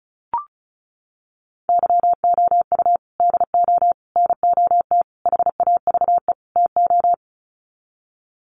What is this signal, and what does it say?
Morse code, heard as a keyed tone, about 34 wpm, 700 Hz. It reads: YOU DO NOT HAVE TO
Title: Sets of 5 Words 35wpm